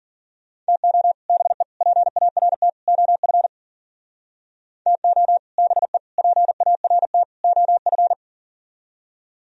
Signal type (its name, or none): Morse code